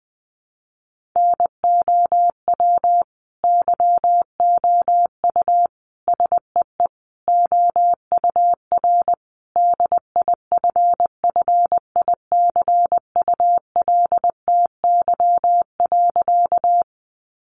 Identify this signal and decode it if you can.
Morse code, heard as a keyed tone, about 20 wpm, 700 Hz. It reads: NOW YOU SEE OUR DIFFICULTY.
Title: Sentences from Top 1500 Words 20wpm